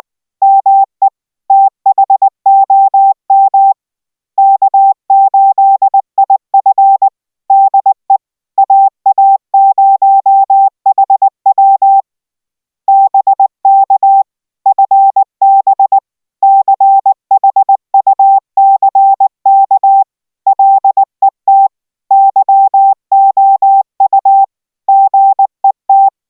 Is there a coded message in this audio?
Yes: Morse code from 0.4 to 26.1 s, heard as a keyed tone, about 19 wpm, 780 Hz. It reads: ME THOM K8IF DE AA0HW BK FB CHUCK LET YOU GET